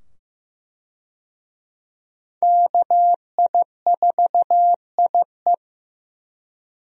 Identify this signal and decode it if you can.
Morse code, heard as a keyed tone, about 15 wpm, 700 Hz. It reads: KI4IE
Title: USA Call Signs - 5 Characters - 15wpm